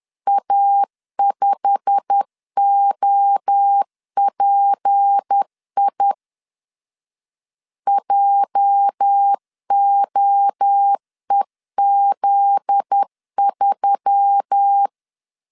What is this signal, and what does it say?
Morse code, heard as a keyed tone, about 10 wpm, 800 Hz. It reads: A5OPI JOEZ3
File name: LEZIONI / 8 YJ45 / 8 YJ45 10wpm 2549